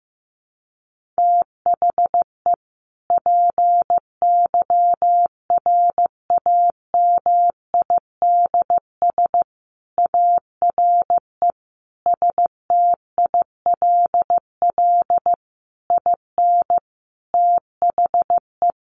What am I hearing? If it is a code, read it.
Morse code, heard as a keyed tone, about 14 wpm, 700 Hz. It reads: THE PYRAMIDS ARE STILL IN THE
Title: A Sound Of Thunder 15wpm